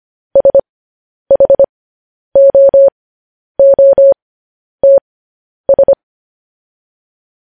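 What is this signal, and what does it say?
Morse code, heard as a keyed tone, about 25 wpm, spaced as for 10 wpm, 550 Hz. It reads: SHOOTS
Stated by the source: Morse code practice sending news headlines at 10 wpm.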